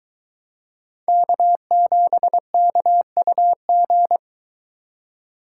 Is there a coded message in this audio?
Yes: Morse code from 1.1 to 4.2 s, heard as a keyed tone, about 22 wpm, 700 Hz. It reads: K7KUG